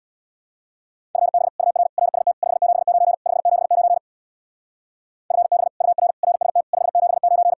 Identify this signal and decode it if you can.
Morse code, heard as a keyed tone, about 62 wpm, 700 Hz. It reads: FB UR RST 579 579 FB UR RST 579
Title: Speed-Racing Common QSO Phrases - 50wpm